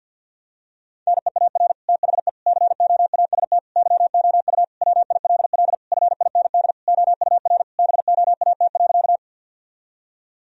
Morse code, heard as a keyed tone, about 50 wpm, 700 Hz. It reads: NEAR THE COAST YOU WILL FIND OUR BOAT.
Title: Sentences from Top 1000 Words - Review Part 4 - 50wpm